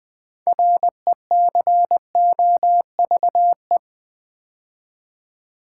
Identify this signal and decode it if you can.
Morse code, heard as a keyed tone, about 20 wpm, 700 Hz. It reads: RECOVE
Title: Build A Word - 2 Letters At A Time - 8-Letter Words - 20wpm